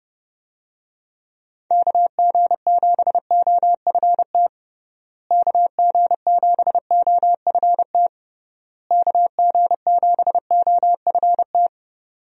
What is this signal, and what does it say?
Morse code, heard as a keyed tone, about 30 wpm, 700 Hz. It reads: KG7OFT KG7OFT KG7OFT
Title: USA Call Signs - 6 Characters - Repeated 3x - 30wpm